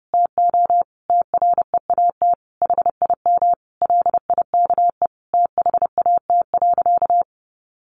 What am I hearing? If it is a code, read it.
Morse code, heard as a keyed tone, about 30 wpm, 700 Hz. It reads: TO TREAT HIM LIKE THAT.
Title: Sentences from Top 1300 Words 30wpm